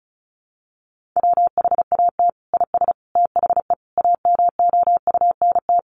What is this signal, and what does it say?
Morse code, heard as a keyed tone, about 34 wpm, 700 Hz. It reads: WHAT IS THE AMOUNT
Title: Sentences from Top 1500 Words 35wpm